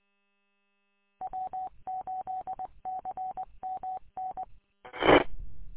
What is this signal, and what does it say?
Morse code, heard as a keyed tone, about 20 wpm, 750 Hz. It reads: W8CMN T
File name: w8cmn- -combined